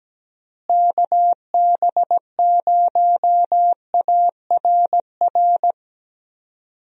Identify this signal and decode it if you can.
Morse code, heard as a keyed tone, about 17 wpm, 700 Hz. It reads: KB0ARR
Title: USA Call Signs - 6 Characters - Repeated 3x - 17wpm